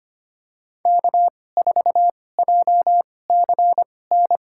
Morse code, heard as a keyed tone, about 25 wpm, spaced as for 18 wpm, 700 Hz. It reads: K4JCN
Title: USA Call Signs - 5 Characters - 2x Character Spacing - 25wpm